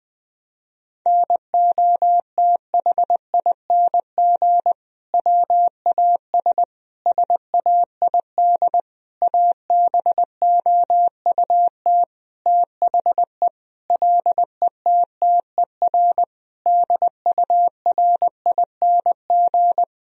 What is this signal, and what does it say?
Morse code, heard as a keyed tone, about 20 wpm, 700 Hz. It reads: NOTHING WAS SAID ABOUT THE LETTER DURING
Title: Sentences from Top 1000 Words - Review Part 2 - 20wpm